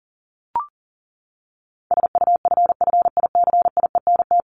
Morse code, heard as a keyed tone, about 37 wpm, 700 Hz. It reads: SUFFICIENT
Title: LY Suffix - Top 100 Words with Base Words 40wpm